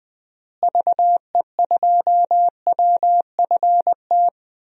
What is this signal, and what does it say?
Morse code, heard as a keyed tone, about 20 wpm, 700 Hz. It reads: VE2WFT
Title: DXCC-100 Call Signs - 6 Characters - Repeated 3x - 20wpm